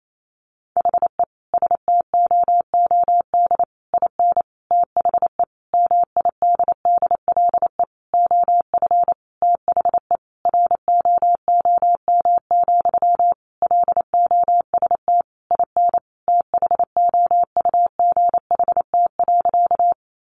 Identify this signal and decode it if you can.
Morse code, heard as a keyed tone, about 28 wpm, 700 Hz. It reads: HE STOOD IN THE MIDDLE OF THE ROOM, LOST IN THOUGHT.
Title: Sentences from Top 1400 Words 28wpm